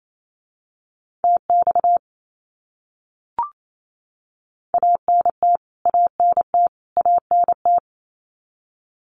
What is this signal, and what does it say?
Morse code, heard as a keyed tone, about 28 wpm, 700 Hz. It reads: TX ANT ANT ANT
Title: Abbreviations Repeated 3x - 28wpm